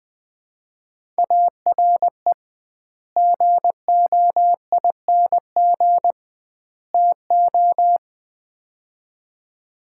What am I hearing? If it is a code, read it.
Morse code, heard as a keyed tone, about 20 wpm, 700 Hz. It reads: ARE GOING TO
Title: 2x Word-Spacing - Sets of 3 Words - 20wpm